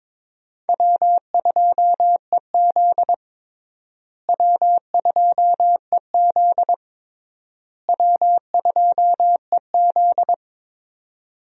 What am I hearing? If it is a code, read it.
Morse code, heard as a keyed tone, about 22 wpm, 700 Hz. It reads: W2EZ W2EZ W2EZ
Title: USA Call Signs - 4 Characters - Repeated 3x - 22wpm